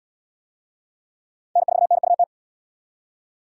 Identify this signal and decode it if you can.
Morse code, heard as a keyed tone, about 77 wpm, 700 Hz. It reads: N4NUT